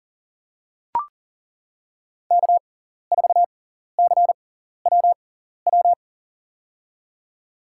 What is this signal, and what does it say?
Morse code, heard as a keyed tone, about 40 wpm, spaced as for 14 wpm, 700 Hz. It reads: K4CWW
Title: USA Call Signs - 6x Character Spacing - 40wpm